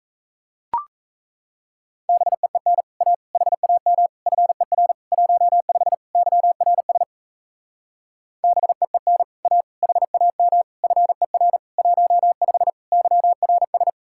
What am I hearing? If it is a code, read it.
Morse code, heard as a keyed tone, about 37 wpm, 700 Hz. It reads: BEEN A HAM FER 15 YRS BEEN A HAM FER 15 YRS